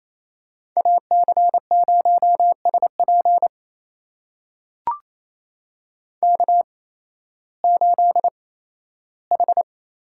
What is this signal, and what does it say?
Morse code, heard as a keyed tone, about 28 wpm, spaced as for 15 wpm, 700 Hz. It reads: AC0SP K8H
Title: USA Call Signs - 8x Character Spacing - 28wpm